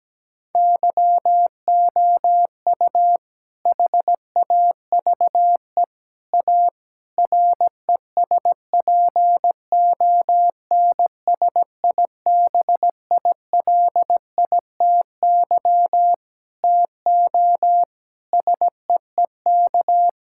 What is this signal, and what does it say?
Morse code, heard as a keyed tone, about 17 wpm, 700 Hz. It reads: YOU HAVE A RESPONSIBILITY TO SEEK
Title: Sentences from Top 1600 Words 17wpm